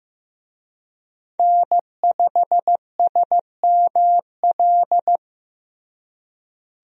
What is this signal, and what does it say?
Morse code, heard as a keyed tone, about 15 wpm, 700 Hz. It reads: N5SML
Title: USA Call Signs - 5 Characters - Repeated 2x - 15wpm